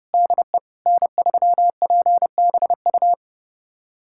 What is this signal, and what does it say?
Morse code, heard as a keyed tone, about 30 wpm, 700 Hz. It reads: DE N3PBU
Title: USA Callsign DE Callsign - 5 Characters - 30wpm